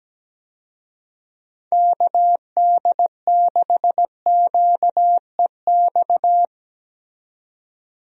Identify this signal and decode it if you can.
Morse code, heard as a keyed tone, about 17 wpm, 700 Hz. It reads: KD6QEX
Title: USA Call Signs - 6 Characters - 17wpm